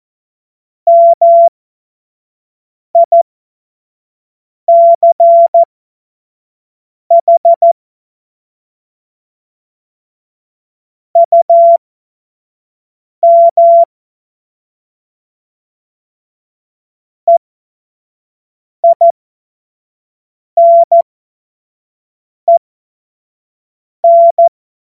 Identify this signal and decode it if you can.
Morse code, heard as a keyed tone, about 14 wpm, spaced as for 5 wpm, 680 Hz. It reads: MICH UM EINEN
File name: Morsekurs / Witze / Witz 18. Juli 2023 25 F70